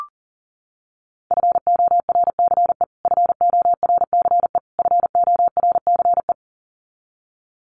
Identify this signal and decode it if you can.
Morse code, heard as a keyed tone, about 40 wpm, 700 Hz. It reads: FORCE FORCE FORCE
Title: Top 500 Words Repeated 40wpm